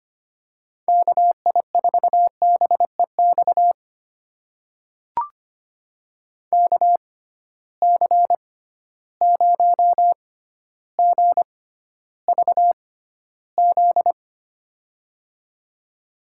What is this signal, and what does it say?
Morse code, heard as a keyed tone, about 25 wpm, spaced as for 13 wpm, 700 Hz. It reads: KI4BEX KC0GVZ
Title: USA Call Signs - 6 Characters - 6x Character Spacing - 25wpm